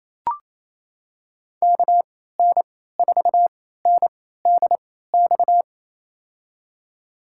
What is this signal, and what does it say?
Morse code, heard as a keyed tone, about 28 wpm, spaced as for 16 wpm, 700 Hz. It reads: KN4NDX
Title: USA Call Signs - 6 Characters - 3x Character Spacing - 28wpm